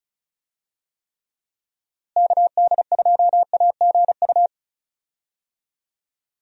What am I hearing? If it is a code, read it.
Morse code, heard as a keyed tone, about 36 wpm, 700 Hz. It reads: KD2AGU
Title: USA Call Signs - 6 Characters - Repeated 2x - 35wpm